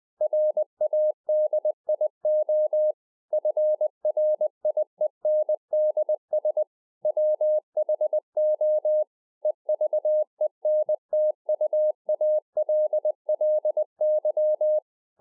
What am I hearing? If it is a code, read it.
Morse code, heard as a keyed tone, about 20 wpm, 610 Hz. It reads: RADIO FRIENDS WHO EVENTUALLY